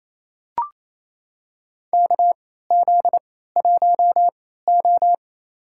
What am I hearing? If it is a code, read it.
Morse code, heard as a keyed tone, about 28 wpm, spaced as for 16 wpm, 700 Hz. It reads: KZ1O
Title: USA Call Signs - 4 Characters - 3x Character Spacing - 28wpm